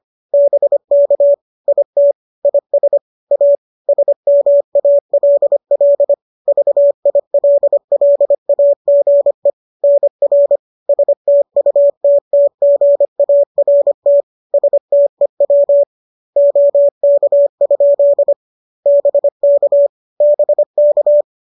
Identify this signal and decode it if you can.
Morse code, heard as a keyed tone, about 25 wpm, 570 Hz. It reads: BK IT IS A SMALL VILLAGE NR STUTTGART STEW OK? BK BK
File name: qso218 25